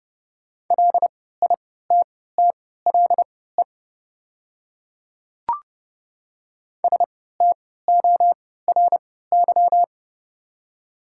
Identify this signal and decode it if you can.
Morse code, heard as a keyed tone, about 30 wpm, spaced as for 17 wpm, 700 Hz. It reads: LITTLE STORY
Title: 3x Character Spacing - Top 300 Words - 30wpm